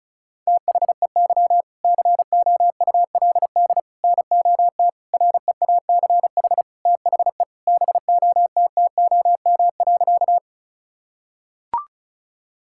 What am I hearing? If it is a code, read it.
Morse code, heard as a keyed tone, about 36 wpm, 700 Hz. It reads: THEY COULD NOT REACH THE BOTTOM.
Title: Sentences from Top 1000 Words - Review Part 6 - 35wpm